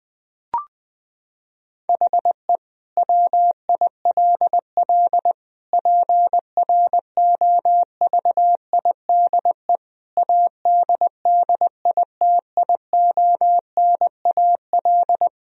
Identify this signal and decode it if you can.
Morse code, heard as a keyed tone, about 21 wpm, 700 Hz. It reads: HE WILL PROVIDE ADDITIONAL
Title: Sentences from Top 1000 Words - Review Part 4 - 20wpm